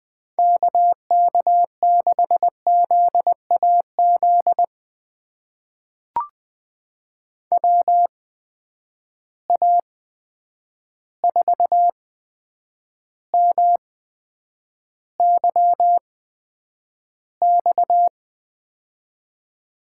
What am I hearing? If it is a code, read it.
Morse code, heard as a keyed tone, about 20 wpm, spaced as for 9 wpm, 700 Hz. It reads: KK6ZAZ WA4MYX
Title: USA Call Signs - 6 Characters - 8x Character Spacing - 20wpm